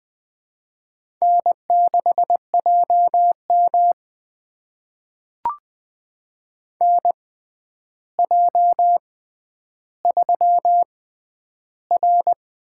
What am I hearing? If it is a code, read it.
Morse code, heard as a keyed tone, about 20 wpm, spaced as for 10 wpm, 700 Hz. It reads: N6JM NJ3R
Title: USA Call Signs - 4 Characters - 6x Character Spacing - 20wpm